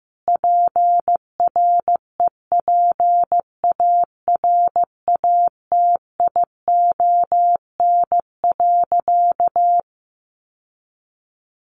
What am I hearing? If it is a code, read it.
Morse code, heard as a keyed tone, about 15 wpm, 700 Hz. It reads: PREPARATION.
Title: Sentences from Top 1400 Words 15wpm